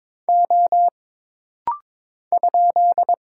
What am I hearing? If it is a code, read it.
Morse code, heard as a keyed tone, about 22 wpm, 700 Hz. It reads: O ?
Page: This rapid-fire timing is geared to push your ICR (Instant Character Recognition) skills.